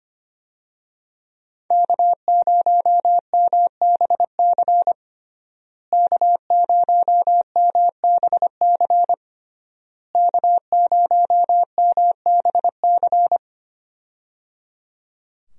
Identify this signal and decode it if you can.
Morse code, heard as a keyed tone, about 25 wpm, 700 Hz. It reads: K0MBC K0MBC K0MBC
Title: USA Call Signs - 5 Characters - Repeated 3x - 25wpm